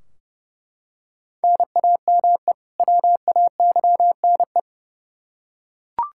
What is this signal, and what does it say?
Morse code, heard as a keyed tone, about 30 wpm, 700 Hz. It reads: NAME WAYNE
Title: Practice QSO - Exchange 1 - Beginning - 30wpm